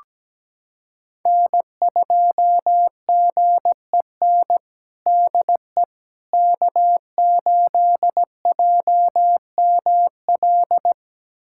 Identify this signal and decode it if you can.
Morse code, heard as a keyed tone, about 17 wpm, 700 Hz. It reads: N2GEN DE K8JML
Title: USA Callsign DE Callsign - 5 Characters - 17wpm